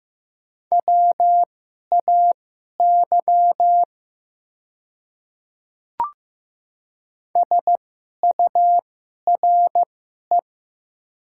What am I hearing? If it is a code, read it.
Morse code, heard as a keyed tone, about 15 wpm, spaced as for 11 wpm, 700 Hz. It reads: WAY SURE
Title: 2x Character Spacing - Top 300 Words - 15wpm